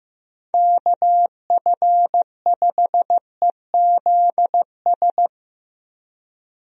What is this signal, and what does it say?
Morse code, heard as a keyed tone, about 15 wpm, 700 Hz. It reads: KF5EZS
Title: USA Call Signs - 6 Characters - Repeated 3x - 15wpm